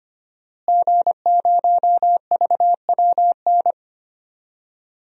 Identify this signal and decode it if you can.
Morse code, heard as a keyed tone, about 25 wpm, 700 Hz. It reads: G0VWN
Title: DXCC-100 Call Signs - 5 Characters - Repeated 2x - 25wpm